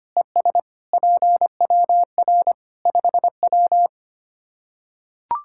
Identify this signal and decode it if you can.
Morse code, heard as a keyed tone, about 25 wpm, 700 Hz. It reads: ES PWR 5W
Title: Practice QSO - Exchange 2 - Info - 25wpm